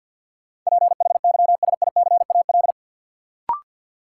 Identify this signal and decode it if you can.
Morse code, heard as a keyed tone, about 51 wpm, 700 Hz. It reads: PHYSICAL
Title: Sentences from Top 1400 Words - New Word Drill Alt - 50wpm